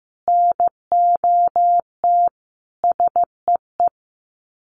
Morse code, heard as a keyed tone, about 15 wpm, 700 Hz. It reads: NOT SEE
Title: Sets of 5 Words Encore2 15wpm